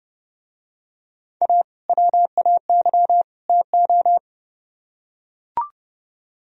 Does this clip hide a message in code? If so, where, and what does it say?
Yes: Morse code from 1.4 to 4.2 s, heard as a keyed tone, about 31 wpm, 700 Hz. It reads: A WAY TO